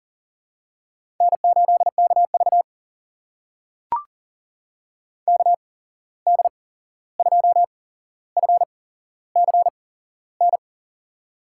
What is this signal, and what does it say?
Morse code, heard as a keyed tone, about 40 wpm, spaced as for 15 wpm, 700 Hz. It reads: N8KV KD2FCN
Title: USA Call Signs - 8x Character Spacing - 40wpm